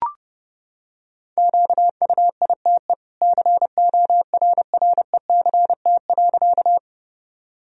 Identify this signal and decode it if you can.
Morse code, heard as a keyed tone, about 30 wpm, 700 Hz. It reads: QUITE CORRECT.
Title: Sentences from Top 1000 Words - Review Part 2 - 30wpm